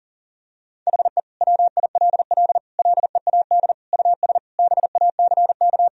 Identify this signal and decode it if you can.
Morse code, heard as a keyed tone, about 40 wpm, 700 Hz. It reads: HE WILL LEAD US BACK
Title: Sentences from Top 1000 Words - Review Part 2 - 40wpm